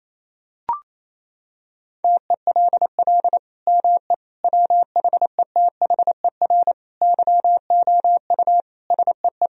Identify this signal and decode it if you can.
Morse code, heard as a keyed tone, about 28 wpm, 700 Hz. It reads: TELL ME WHETHER YOU SEE
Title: Sentences from Top 1000 Words - Review Part 3 - 28wpm